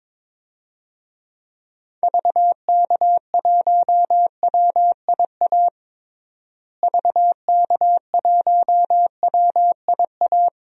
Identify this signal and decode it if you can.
Morse code, heard as a keyed tone, about 22 wpm, 700 Hz. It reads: VK1WIA VK1WIA
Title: DXCC-100 Call Signs - 6 Characters - Repeated 3x - 22wpm